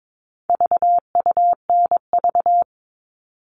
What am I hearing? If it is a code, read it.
Morse code, heard as a keyed tone, about 22 wpm, 700 Hz. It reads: VUNV
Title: All Cut Numbers - 4-Digit 22wpm